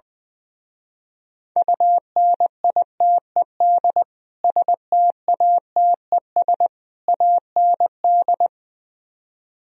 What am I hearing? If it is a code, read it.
Morse code, heard as a keyed tone, about 20 wpm, 700 Hz. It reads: UNITED STATES AND
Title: Speed-Racing - Sets of 3 Words 20wpm